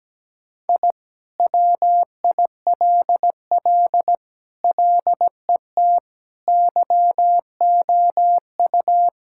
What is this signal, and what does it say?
Morse code, heard as a keyed tone, about 17 wpm, 700 Hz. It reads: I WILL LET YOU
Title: Sentences from Top 1000 Words - Review Part 5 - 17wpm